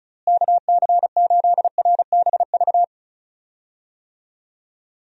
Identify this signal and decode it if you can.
Morse code, heard as a keyed tone, about 36 wpm, 700 Hz. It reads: KC8RBV
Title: USA Call Signs - 6 Characters - Repeated 2x - 35wpm